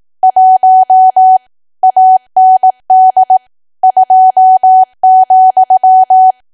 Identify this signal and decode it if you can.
Morse code, heard as a keyed tone, about 18 wpm, 750 Hz. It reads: 1 AND 2,
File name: cwa / W1AW-Level2 / 18WPM